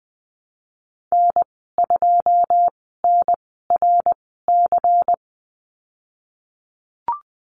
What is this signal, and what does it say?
Morse code, heard as a keyed tone, about 20 wpm, 700 Hz. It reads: N 2 N R C
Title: 154 Lesson 25g - Letter p - Call Signs 20wpm